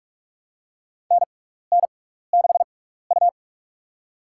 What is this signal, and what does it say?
Morse code, heard as a keyed tone, about 47 wpm, spaced as for 16 wpm, 700 Hz. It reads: NN6U
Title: USA Call Signs - 4 Characters - 6x Character Spacing - 45wpm